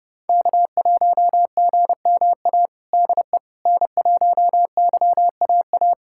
Morse code, heard as a keyed tone, about 30 wpm, 700 Hz. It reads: K1GMA DE N1YAA
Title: USA Callsign DE Callsign - 5 Characters - 30wpm